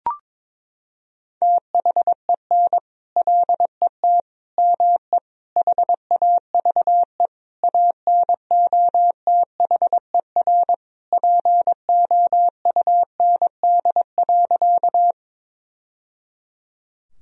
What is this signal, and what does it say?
Morse code, heard as a keyed tone, about 22 wpm, 700 Hz. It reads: THEN LET ME HAVE ANOTHER POUND.
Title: Sentences from Top 1000 Words - Review Part 2 - 22wpm